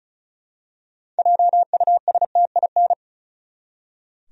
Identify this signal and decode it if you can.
Morse code, heard as a keyed tone, about 34 wpm, 700 Hz. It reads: JUSTIN
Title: USA Common Names - 35wpm